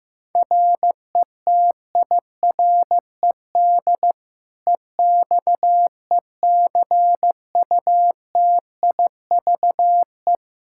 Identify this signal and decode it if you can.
Morse code, heard as a keyed tone, about 15 wpm, 700 Hz. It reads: RETIRED EXECUTIVE
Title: Practice QSO - Exchange 3 - More Info - 15wpm